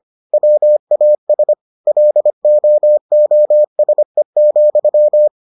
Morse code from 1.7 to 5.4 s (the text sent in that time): LOOSE,